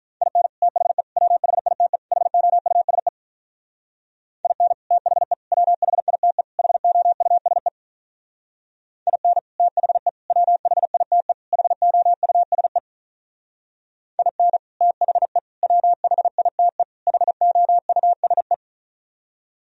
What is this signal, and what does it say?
Morse code, heard as a keyed tone, about 45 wpm, 700 Hz. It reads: IN THE WHITE HOUSE IN THE WHITE HOUSE IN THE WHITE HOUSE IN THE WHITE HOUSE